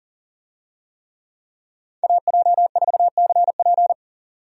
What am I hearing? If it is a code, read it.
Morse code, heard as a keyed tone, about 40 wpm, 700 Hz. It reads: AJ4CP
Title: USA Call Signs - 5 Characters - 40wpm